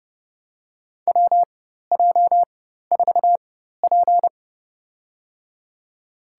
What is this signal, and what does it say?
Morse code, heard as a keyed tone, about 29 wpm, spaced as for 14 wpm, 700 Hz. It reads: WJ4P
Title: USA Call Signs - 4 Characters - 4x Character Spacing - 30wpm